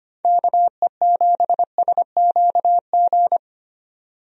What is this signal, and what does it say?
Morse code, heard as a keyed tone, about 25 wpm, 700 Hz. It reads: KE7SQG
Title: USA Call Signs - 6 Characters - 25wpm